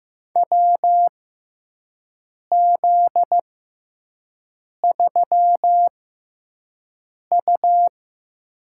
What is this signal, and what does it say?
Morse code, heard as a keyed tone, about 15 wpm, spaced as for 5 wpm, 700 Hz. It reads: WZ3U
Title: USA Call Signs - 4 Characters - 6x Character Spacing - 15wpm